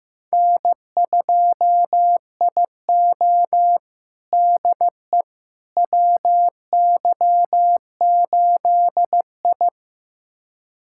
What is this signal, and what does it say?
Morse code, heard as a keyed tone, about 15 wpm, 700 Hz. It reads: N2IO DE WY8I
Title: USA Callsign DE Callsign - 4 Characters - 15wpm